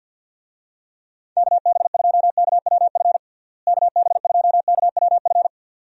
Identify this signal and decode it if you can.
Morse code, heard as a keyed tone, about 52 wpm, 700 Hz. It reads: KB2KWF KB2KWF
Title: USA Call Signs - 6 Characters - Repeated 2x - 50wpm